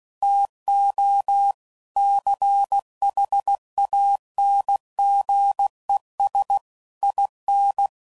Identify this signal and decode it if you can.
Morse code, heard as a keyed tone, about 16 wpm, 800 Hz. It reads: TO CHANGES IN
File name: CW Code Groups @15WPM / Technical Subjects / Reactance